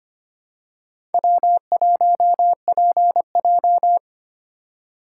Morse code, heard as a keyed tone, about 25 wpm, 700 Hz. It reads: W1PJ
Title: USA Call Signs - 4 Characters - 4x Character Spacing - 25wpm